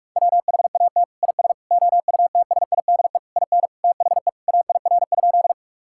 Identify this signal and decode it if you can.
Morse code, heard as a keyed tone, about 41 wpm, 700 Hz. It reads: WHAT IS OUTSIDE IN THE AIR?
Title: Sentences from Top 1000 Words - Review Part 6 - 45wpm